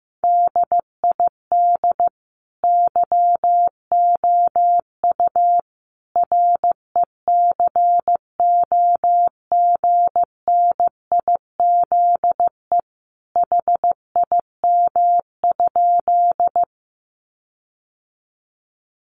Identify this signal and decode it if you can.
Morse code, heard as a keyed tone, about 15 wpm, 700 Hz. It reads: DID YOU RECOGNIZE HIM?
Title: Sentences from Top 1400 Words 15wpm